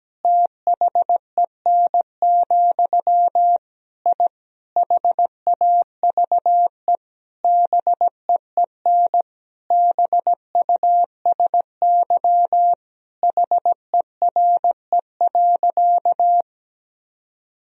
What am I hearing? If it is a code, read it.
Morse code, heard as a keyed tone, about 17 wpm, 700 Hz. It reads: THEN, I HAVE BEEN BUSY HERE.
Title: Sentences from Top 1000 Words - Review Part 3 - 17wpm